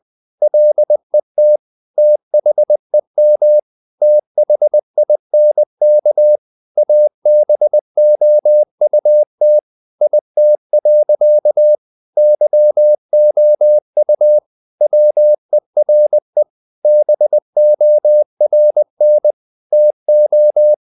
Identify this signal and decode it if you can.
Morse code, heard as a keyed tone, about 20 wpm, 600 Hz. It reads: LET THEM THINK ABOUT IT. YOU WERE BORN TO
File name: the old man and the sea part 2 20wpm